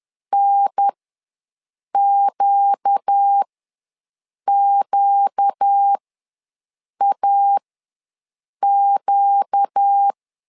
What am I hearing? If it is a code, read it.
Morse code, heard as a keyed tone, about 10 wpm, spaced as for 6 wpm, 800 Hz. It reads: NQQAQ
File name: LEZIONI / 4 Q ws 10wpm 2306